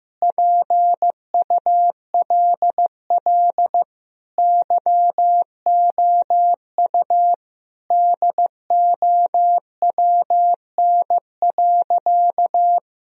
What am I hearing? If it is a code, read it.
Morse code, heard as a keyed tone, about 16 wpm, 700 Hz. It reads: PULL YOU DOWN.
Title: Sentences from Top 1000 Words - Review Part 2 - 15wpm